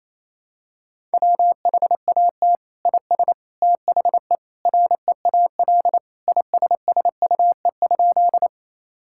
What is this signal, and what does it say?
Morse code, heard as a keyed tone, about 27 wpm, 700 Hz. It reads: WHAT IS THE REAL ISSUE?
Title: Sentences from Top 1000 Words - Review Part 6 - 28wpm